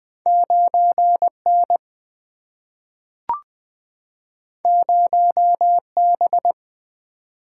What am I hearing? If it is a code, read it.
Morse code, heard as a keyed tone, about 20 wpm, 700 Hz. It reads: 9 N 0 B
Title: Single-Letter Single-Number 20wpm